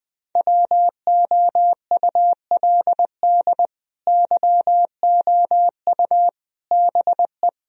Morse code, heard as a keyed tone, about 21 wpm, 700 Hz. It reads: WOULD YOU BE
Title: Sentences from Top 1000 Words - Review Part 2 - 20wpm